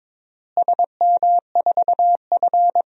Morse code, heard as a keyed tone, about 22 wpm, 700 Hz. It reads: SM4F
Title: DXCC-100 Call Signs - 4 Characters - Repeated 2x - 22wpm